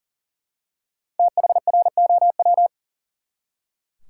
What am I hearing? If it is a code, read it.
Morse code, heard as a keyed tone, about 37 wpm, 700 Hz. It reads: THROW
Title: Five Letter Words - 40wpm